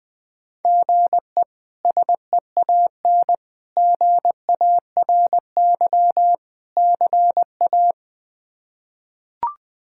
Morse code, heard as a keyed tone, about 21 wpm, 700 Hz. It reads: GE SEAN GARY CA
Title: K1USN SST QSO Practice - 20wpm